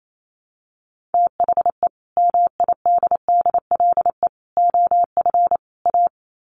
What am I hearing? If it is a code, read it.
Morse code, heard as a keyed tone, about 28 wpm, 700 Hz. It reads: THE MIDDLE OF A
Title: Sets of 4 Words 28wpm